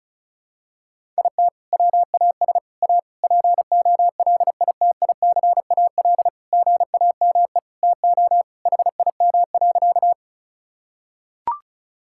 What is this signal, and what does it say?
Morse code, heard as a keyed tone, about 36 wpm, 700 Hz. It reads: IT WAS A POLITICAL GAME TO HIM.
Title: Sentences from Top 1000 Words - Review Part 5 - 35wpm